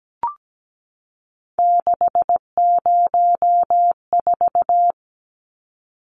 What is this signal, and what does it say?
Morse code, heard as a keyed tone, about 17 wpm, 700 Hz. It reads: 604
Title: Numbers - 3-Digit 17wpm